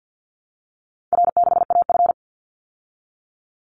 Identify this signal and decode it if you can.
Morse code, heard as a keyed tone, about 48 wpm, 700 Hz. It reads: R6AF
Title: Call Signs Repeated 2x - 50wpm